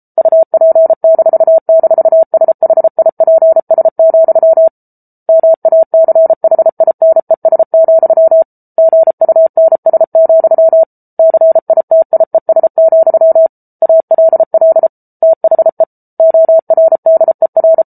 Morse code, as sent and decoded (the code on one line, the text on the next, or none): ..- .--. -....- -....- ... .... .. .--. ... --..-- / -- .- -.-. .... .. -. ... --..-- / --. ..- -. ... --..-- / -.-. .. - .. ... --..-- / .- .-.. .-.. / - .... / --- .-. -.. .-.
UP--SHIPS, MACHINES, GUNS, CITIES, ALL THE ORDER